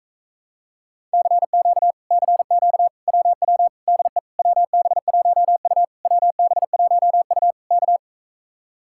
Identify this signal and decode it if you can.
Morse code, heard as a keyed tone, about 44 wpm, 700 Hz. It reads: CQ CQ WW DE WB1U WB1U K